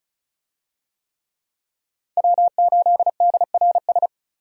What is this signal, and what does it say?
Morse code, heard as a keyed tone, about 34 wpm, 700 Hz. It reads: W8DRS
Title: USA Call Signs - 5 Characters - 35wpm